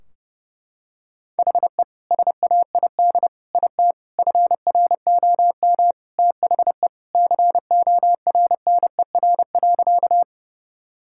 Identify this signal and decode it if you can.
Morse code, heard as a keyed tone, about 32 wpm, 700 Hz. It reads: HE SAID IT FROM THE CORNER.
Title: Sentences from Top 1600 Words 30wpm